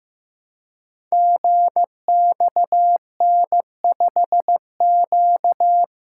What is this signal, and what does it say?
Morse code, heard as a keyed tone, about 15 wpm, 700 Hz. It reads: GXN5Q
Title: Code Groups - 5 Letter:Numbers - 15wpm